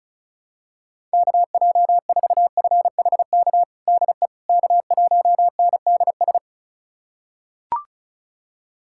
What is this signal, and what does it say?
Morse code, heard as a keyed tone, about 36 wpm, 700 Hz. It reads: KJ4FHK DE K1NDS
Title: USA Callsign DE Callsign 35wpm